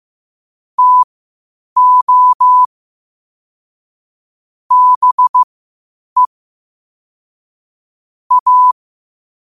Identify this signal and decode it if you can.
Morse code, heard as a keyed tone, about 15 wpm, spaced as for 8 wpm, 1000 Hz. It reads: TO BE A
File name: qotdfiles / 8wpm